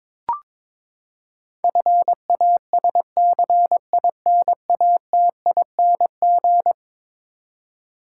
Morse code, heard as a keyed tone, about 22 wpm, 700 Hz. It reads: FASCINATING
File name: practice / 22wpm / Eleven Letter Words - 22wpm